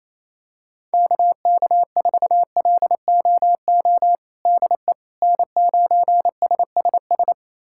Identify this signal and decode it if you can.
Morse code, heard as a keyed tone, about 28 wpm, 700 Hz. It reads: KK4LOO DE N9SSS
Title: USA Callsign DE Callsign 28wpm